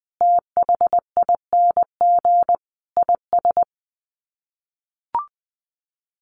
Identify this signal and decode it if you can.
Morse code, heard as a keyed tone, about 19 wpm, 700 Hz. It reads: THING IS
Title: Sets of 5 Words Encore 20wpm